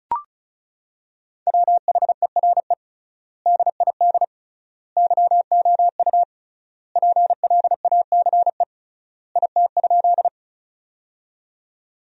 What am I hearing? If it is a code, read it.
Morse code, heard as a keyed tone, about 36 wpm, 700 Hz. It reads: WHERE DID YOU PLACE IT?
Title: 3x Word-Spacing - Sentences from Top 200 Words - 35wpm